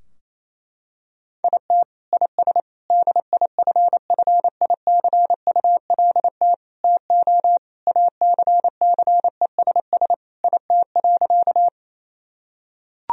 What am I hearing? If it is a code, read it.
Morse code, heard as a keyed tone, about 27 wpm, 700 Hz. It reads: IT IS DIFFICULT TO ACCESS IT.
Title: Sentences from Top 1000 Words - Review Part 4 - 28wpm